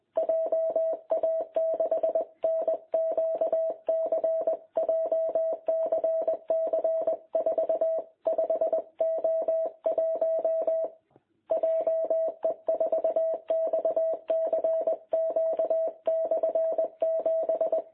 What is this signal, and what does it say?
Morse code, heard as a keyed tone, about 21 wpm, 660 Hz. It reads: JA6NQC JCC 45O1 JE4XCQ/7
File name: jh8cbh / 2021cw